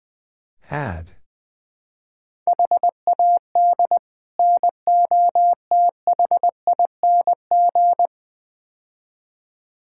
Morse code, heard as a keyed tone, about 21 wpm, 700 Hz. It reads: HAD NOTHING
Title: Build A Phrase - 5-Word Phrases - 20wpm